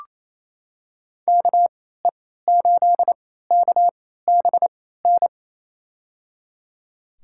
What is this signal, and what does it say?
Morse code, heard as a keyed tone, about 28 wpm, spaced as for 16 wpm, 700 Hz. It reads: KE8KBN
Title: USA Call Signs - 6 Characters - 3x Character Spacing - 28wpm